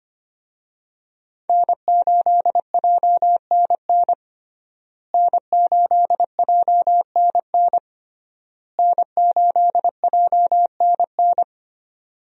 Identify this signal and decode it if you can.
Morse code, heard as a keyed tone, about 25 wpm, 700 Hz. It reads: N8JNN N8JNN N8JNN
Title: USA Call Signs - 5 Characters - Repeated 3x - 25wpm